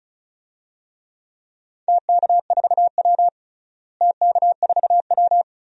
Morse code, heard as a keyed tone, about 36 wpm, 700 Hz. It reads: TK4W TK4W
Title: DXCC-100 Call Signs - 4 Characters - Repeated 2x - 35wpm